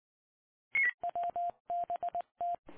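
Morse code, heard as a keyed tone, about 20 wpm, 700 Hz. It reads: WBT
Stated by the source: Morse ident of the PAPA System analog repeaters, WD6FZA.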